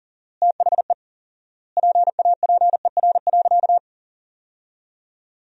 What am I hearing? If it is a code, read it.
Morse code, heard as a keyed tone, about 40 wpm, 700 Hz. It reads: THE PAPER.
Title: 4x Word-Spacing - Sentences from Top 300 Words - 40wpm